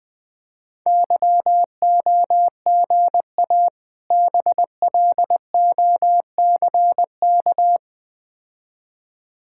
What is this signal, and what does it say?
Morse code, heard as a keyed tone, about 20 wpm, 700 Hz. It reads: YOGA BLOCK
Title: Difficult Words with Hints - 20wpm